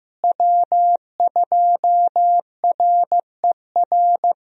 Morse code, heard as a keyed tone, about 15 wpm, 700 Hz. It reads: W2RER
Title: USA Call Signs - 5 Characters - Repeated 2x - 15wpm